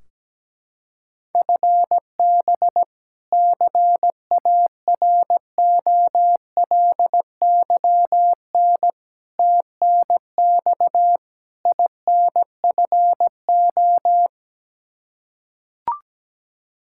Morse code, heard as a keyed tone, about 17 wpm, 700 Hz. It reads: FB CAROLYN TNX INFO
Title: Practice QSO - Exchange 3 - More Info - 17wpm